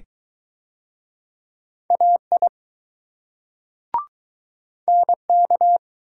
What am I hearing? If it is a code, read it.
Morse code, heard as a keyed tone, about 21 wpm, 700 Hz. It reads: AI NK